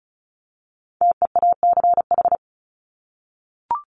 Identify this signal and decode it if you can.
Morse code, heard as a keyed tone, about 34 wpm, 700 Hz. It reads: TEACH
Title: Top 500 Words Repeated 35wpm